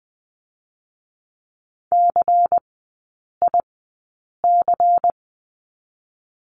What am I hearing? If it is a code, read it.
Morse code, heard as a keyed tone, about 20 wpm, 700 Hz. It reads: C I C
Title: 114 Lesson 19e - Letter w - Three Characters 20wpm